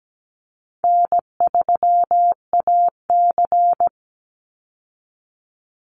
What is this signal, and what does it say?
Morse code, heard as a keyed tone, about 17 wpm, 700 Hz. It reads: N3AC
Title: Easy Call Signs Repeated 3x - 17wpm